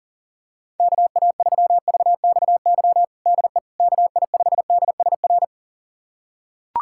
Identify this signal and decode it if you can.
Morse code, heard as a keyed tone, about 40 wpm, 700 Hz. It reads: KA3VXY DE KI5DSR
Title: USA Callsign DE Callsign 40wpm